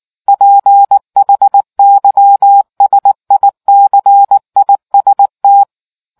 Morse code, heard as a keyed tone, about 19 wpm, 800 Hz. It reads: PHYSICIST